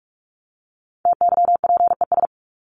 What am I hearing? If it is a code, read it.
Morse code, heard as a keyed tone, about 44 wpm, 700 Hz. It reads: TYPES
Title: ES:S Suffix - Top 100 Words with Base Words 45wpm